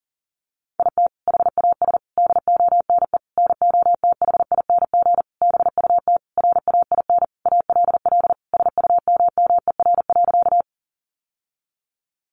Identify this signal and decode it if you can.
Morse code, heard as a keyed tone, about 40 wpm, 700 Hz. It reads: IT HAS DONE NOTHING BUT RAIN ALL SUMMER.
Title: Sentences from Top 1100 Words 40wpm